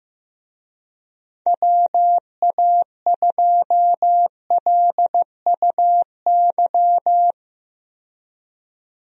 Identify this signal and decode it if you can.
Morse code, heard as a keyed tone, about 15 wpm, 700 Hz. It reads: WA2LUY
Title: USA Call Signs - 6 Characters - 15wpm